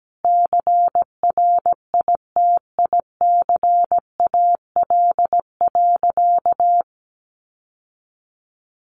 Morse code, heard as a keyed tone, about 17 wpm, 700 Hz. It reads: CRITICAL.
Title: Sentences from Top 1100 Words 17wpm